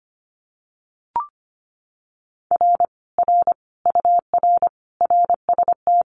Morse code, heard as a keyed tone, about 25 wpm, 700 Hz. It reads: R R UR RST
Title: Common QSO Phrases Encore 25wpm